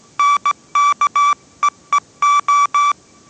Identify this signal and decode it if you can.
Morse code, heard as a keyed tone, about 17 wpm, 1160 Hz. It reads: NKEEO
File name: 18wpm